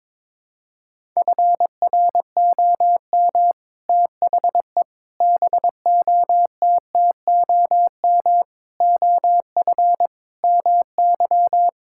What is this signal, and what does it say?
Morse code, heard as a keyed tone, about 21 wpm, 700 Hz. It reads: FROM THE BOTTOM OF MY
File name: practice / 22wpm / Sets of 6 Words - 22wpm